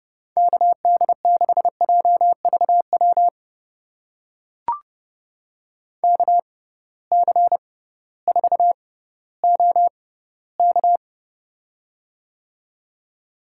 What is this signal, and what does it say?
Morse code, heard as a keyed tone, about 32 wpm, spaced as for 17 wpm, 700 Hz. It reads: KD6JVW KC4OK
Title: USA Call Signs - 6x Character Spacing - 30wpm